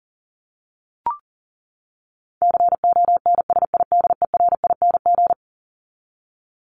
Morse code, heard as a keyed tone, about 37 wpm, 700 Hz. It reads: CONSIDERING
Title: ING Suffix - Top 100 Words with Base Words 40wpm